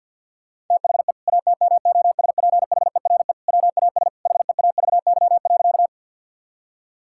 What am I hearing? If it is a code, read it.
Morse code, heard as a keyed tone, about 50 wpm, 700 Hz. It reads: THE ATMOSPHERE WAS HEAVY.
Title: Sentences from Top 1600 Words 50wpm